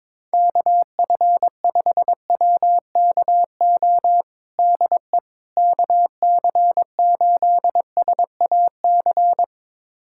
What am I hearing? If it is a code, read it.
Morse code, heard as a keyed tone, about 22 wpm, 700 Hz. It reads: KF5WKO DE KC8SAC
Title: USA Callsign DE Callsign 22wpm